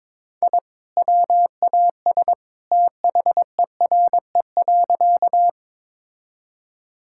Morse code, heard as a keyed tone, about 23 wpm, 700 Hz. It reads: I WAS THERE.
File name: practice / 22wpm / Sentences with 2-Word Phrases - 22wpm